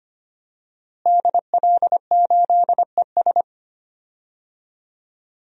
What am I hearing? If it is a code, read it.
Morse code, heard as a keyed tone, about 25 wpm, 700 Hz. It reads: DL8ES
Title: DXCC-100 Call Signs - 5 Characters - Repeated 3x - 25wpm